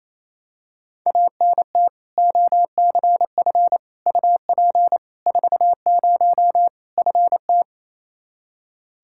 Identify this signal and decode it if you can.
Morse code, heard as a keyed tone, about 28 wpm, 700 Hz. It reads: ANT OCF UP 40 FT
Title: K1USN SST QSO Practice 28wpm